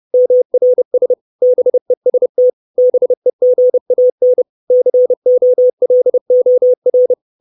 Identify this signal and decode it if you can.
Morse code, heard as a keyed tone, about 30 wpm, 500 Hz. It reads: MRS BEST BEGAN COLOR
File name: 632 Words 30WPM